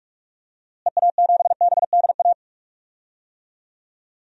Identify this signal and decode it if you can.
Morse code, heard as a keyed tone, about 41 wpm, 690 Hz. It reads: EA7BDA
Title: DXCC-100 Call Signs - 6 Characters - Repeated 2x - 45wpm